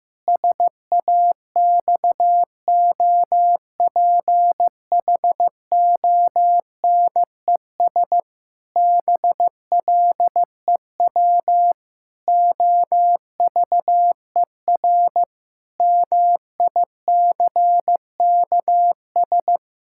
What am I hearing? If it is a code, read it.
Morse code, heard as a keyed tone, about 15 wpm, 700 Hz. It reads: SAXOPHONES BLEW OVER MICKS
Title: Pangram Challenge v2 - 15wpm